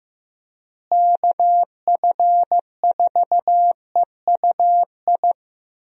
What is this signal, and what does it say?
Morse code, heard as a keyed tone, about 15 wpm, 700 Hz. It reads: KF4EUI
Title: USA Call Signs - 6 Characters - Repeated 2x - 15wpm